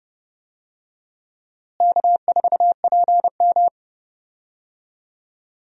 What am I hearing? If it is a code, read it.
Morse code, heard as a keyed tone, about 31 wpm, 700 Hz. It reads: K4PM
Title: USA Call Signs - 4 Characters - Repeated 3x - 30wpm